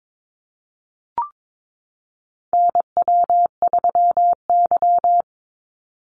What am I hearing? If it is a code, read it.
Morse code, heard as a keyed tone, about 22 wpm, 690 Hz. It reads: NW3Y
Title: Call Signs Repeated 2x - 22wpm